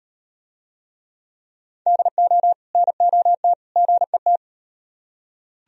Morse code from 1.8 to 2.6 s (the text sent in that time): DO